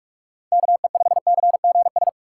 Morse code, heard as a keyed tone, about 44 wpm, 700 Hz. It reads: KE5CGS